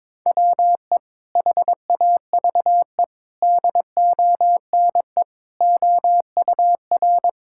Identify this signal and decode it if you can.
Morse code, heard as a keyed tone, about 22 wpm, 700 Hz. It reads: WE HAVE DONE OUR
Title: Sentences from Top 1000 Words - Review Part 6 - 22wpm